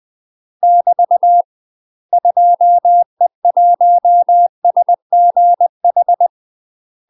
Morse code, heard as a keyed tone, about 20 wpm, 700 Hz. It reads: <BT> 2E1SGH
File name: UK MASTER QSOs TRAINING / 20WPM / UK MASTER QSOs 0051